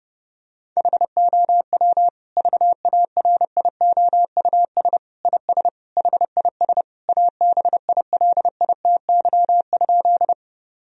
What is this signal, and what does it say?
Morse code, heard as a keyed tone, about 30 wpm, 700 Hz. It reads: HOW VARIOUS IS HIS ABILITY?
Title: Sentences from Top 1000 Words - Review Part 4 - 30wpm